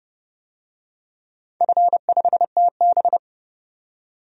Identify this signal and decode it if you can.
Morse code, heard as a keyed tone, about 30 wpm, 700 Hz. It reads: F5TB
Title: DXCC-100 Call Signs - 4 Characters - 30wpm